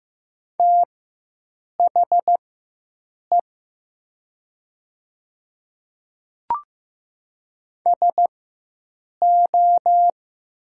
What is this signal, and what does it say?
Morse code, heard as a keyed tone, about 15 wpm, spaced as for 7 wpm, 700 Hz. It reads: THE SO
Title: 4x Character Spacing - Top 100 Words - 15wpm